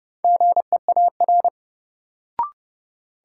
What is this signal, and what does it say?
Morse code, heard as a keyed tone, about 30 wpm, 700 Hz. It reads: GEAR
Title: Sentences from Top 1400 Words - New Word Drill - 30wpm